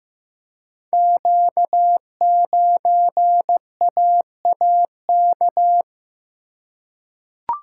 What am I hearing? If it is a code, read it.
Morse code, heard as a keyed tone, about 16 wpm, 700 Hz. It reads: Q9AAK
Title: Code Groups - 5 Letter:Numbers - 15wpm